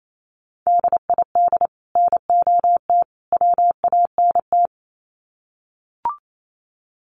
Morse code, heard as a keyed tone, about 28 wpm, 700 Hz. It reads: DID NOT WANT
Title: Sets of 3 Words 28wpm